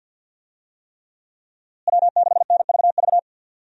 Morse code, heard as a keyed tone, about 46 wpm, 700 Hz. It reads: W6NVV
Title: USA Call Signs - 5 Characters - Repeated 3x - 50wpm